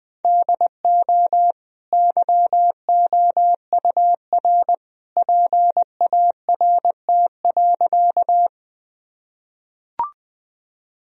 Morse code, heard as a keyed tone, about 20 wpm, 700 Hz. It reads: DO YOUR PART.
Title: Sentences from Top 1000 Words - Review Part 1 - 20wpm